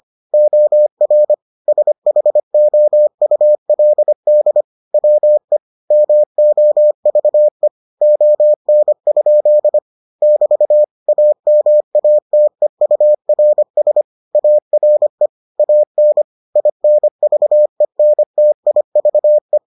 Morse code, heard as a keyed tone, about 25 wpm, 600 Hz. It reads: OR SHOULD WE MOVE ON? <BT> AMATEURS ARE AN INVENTIVE